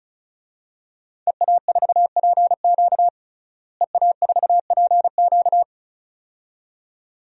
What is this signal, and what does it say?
Morse code, heard as a keyed tone, about 34 wpm, 700 Hz. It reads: EA4PQ EA4PQ
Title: DXCC-100 Call Signs - 5 Characters - Repeated 2x - 35wpm